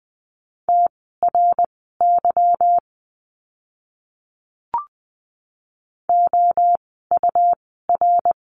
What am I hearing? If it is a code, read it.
Morse code, heard as a keyed tone, about 20 wpm, spaced as for 14 wpm, 700 Hz. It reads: TRY OUR
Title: 160 Lesson 26f - Letter g - Words 20wpm